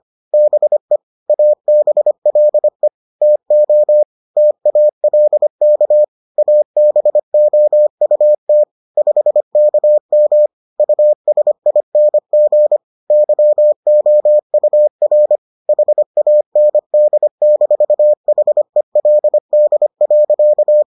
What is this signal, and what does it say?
Morse code, heard as a keyed tone, about 25 wpm, 600 Hz. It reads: BE ABLE TO TALK ABOUT 5KM USING YOUR HAND-HELD.